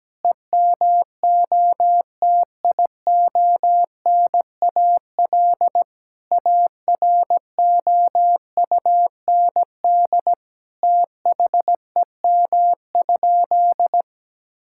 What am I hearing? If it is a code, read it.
Morse code, heard as a keyed tone, about 17 wpm, 700 Hz. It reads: EMOTIONAL AROUND THEM?
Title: Sentences from Top 1000 Words - Review Part 5 - 17wpm